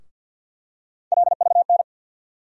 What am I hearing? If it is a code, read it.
Morse code, heard as a keyed tone, about 49 wpm, 700 Hz. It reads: LVN
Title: USA Call Sign Postfixes - 3 Letters 50wpm